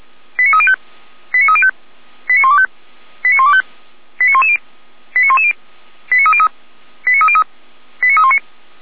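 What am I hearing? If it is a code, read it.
Five-tone selective call (ZVEI), heard as quick runs of five tones, tones of about 70 ms. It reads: 89305 89305 89125 89125 89110 89110 89303 89303 89319
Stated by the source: ZVEI five-tone call received by SDR, from the audio server of Feuerwehr Orlamünde (f85360).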